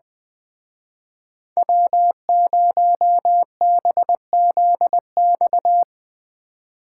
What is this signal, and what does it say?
Morse code, heard as a keyed tone, about 25 wpm, 700 Hz. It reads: W0BZX
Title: Speed Racing - USA 1x3 Call Signs 20wpm